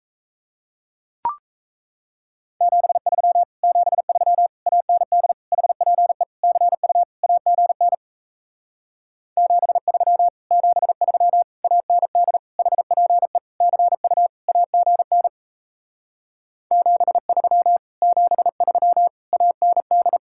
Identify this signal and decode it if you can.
Morse code, heard as a keyed tone, about 37 wpm, 700 Hz. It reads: 73 73 AND HPE CU AGN 73 73 AND HPE CU AGN 73 73 AND